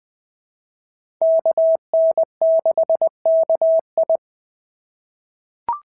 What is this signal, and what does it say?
Morse code, heard as a keyed tone, about 20 wpm, 690 Hz. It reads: KN6KI
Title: POTA QSO Practice - 20wpm